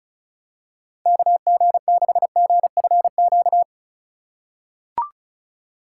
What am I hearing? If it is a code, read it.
Morse code, heard as a keyed tone, about 34 wpm, 700 Hz. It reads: KG6GFQ
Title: USA Call Signs - 6 Characters - 4x Character Spacing - 35wpm